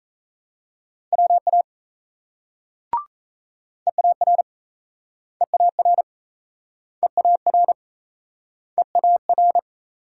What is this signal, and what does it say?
Morse code, heard as a keyed tone, about 39 wpm, 700 Hz. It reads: WA EAR EAR EAR EAR